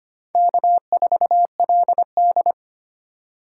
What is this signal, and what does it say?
Morse code, heard as a keyed tone, about 25 wpm, 700 Hz. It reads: K4LD
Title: USA Call Signs - 4 Characters - 2x Character Spacing - 25wpm